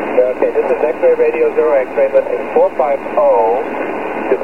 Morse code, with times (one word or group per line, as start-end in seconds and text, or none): none